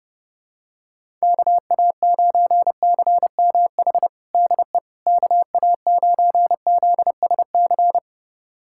Morse code, heard as a keyed tone, about 30 wpm, 700 Hz. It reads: KA9CMH DE KA9ZSC
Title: USA Callsign DE Callsign 30wpm